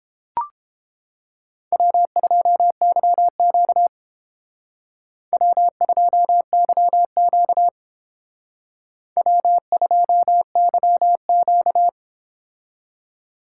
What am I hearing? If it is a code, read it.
Morse code, heard as a keyed tone, about 32 wpm, 700 Hz. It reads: W2YQ W2YQ W2YQ